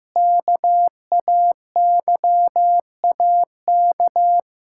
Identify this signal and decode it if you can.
Morse code, heard as a keyed tone, about 15 wpm, 700 Hz. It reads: KAYAK
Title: Pangram Challenge v2 - 15wpm